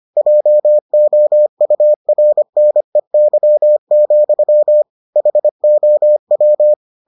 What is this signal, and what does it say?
Morse code, heard as a keyed tone, about 25 wpm, 600 Hz. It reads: JOURNEY, HOW